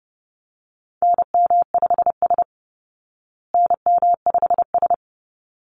Morse code, heard as a keyed tone, about 30 wpm, 700 Hz. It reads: NM5S NM5S
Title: Call Signs Repeated 3x - 30wpm